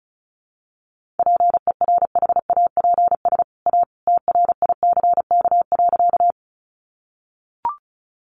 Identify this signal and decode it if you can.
Morse code, heard as a keyed tone, about 34 wpm, 700 Hz. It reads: PERHAPS A TRICK.
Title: Sentences from Top 1500 Words 35wpm